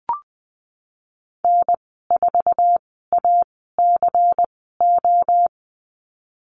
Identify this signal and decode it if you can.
Morse code, heard as a keyed tone, about 20 wpm, spaced as for 15 wpm, 700 Hz. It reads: N4ACO
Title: 161 Lesson 26g - Letter g - Call Signs 20wpm